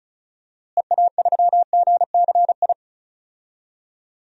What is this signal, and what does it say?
Morse code, heard as a keyed tone, about 34 wpm, 700 Hz. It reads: EA3GCI
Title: DXCC-100 Call Signs - 6 Characters - Repeated 2x - 35wpm